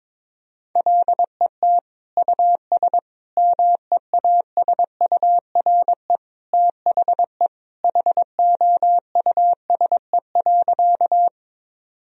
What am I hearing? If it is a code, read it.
Morse code, heard as a keyed tone, about 22 wpm, 700 Hz. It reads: LET US MEASURE THE HOUSE.
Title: Sentences from Top 1000 Words - Review Part 2 - 22wpm